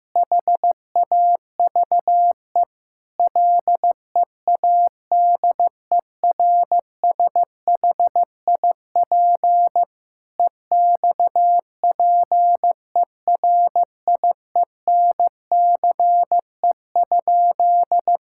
Morse code, heard as a keyed tone, about 16 wpm, 700 Hz. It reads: HAVE LEADERSHIP EXPERIENCE?
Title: Sentences from Top 1600 Words 15wpm